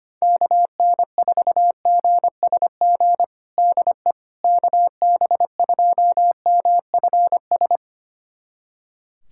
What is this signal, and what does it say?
Morse code, heard as a keyed tone, about 25 wpm, 700 Hz. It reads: KN4GSG DE KB2MFS
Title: USA Callsign DE Callsign - 6 Characters - 25wpm